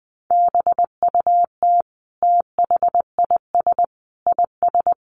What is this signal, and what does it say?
Morse code, heard as a keyed tone, about 20 wpm, 700 Hz. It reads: BUT THIS IS
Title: Sentences from Top 1100 Words 20wpm